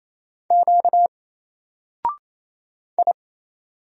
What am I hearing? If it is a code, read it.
Morse code, heard as a keyed tone, about 28 wpm, 700 Hz. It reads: Q I
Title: Single Letter-Number Alt 28wpm